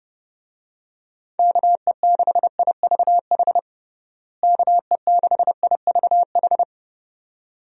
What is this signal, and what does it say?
Morse code, heard as a keyed tone, about 30 wpm, 700 Hz. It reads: KE6IVH KE6IVH
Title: USA Call Signs - 6 Characters - Repeated 2x - 30wpm